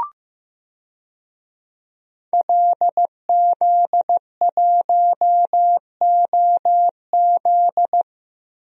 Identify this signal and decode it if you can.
Morse code, heard as a keyed tone, about 15 wpm, 700 Hz. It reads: LZ1OZ
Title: DXCC-100 Call Signs - 5 Characters - Repeated 2x - 15wpm